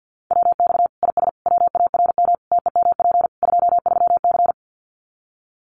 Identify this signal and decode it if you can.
Morse code, heard as a keyed tone, about 50 wpm, 700 Hz. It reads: WX IS WARM TEMP 23C
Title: Common QSO Phrases 50wpm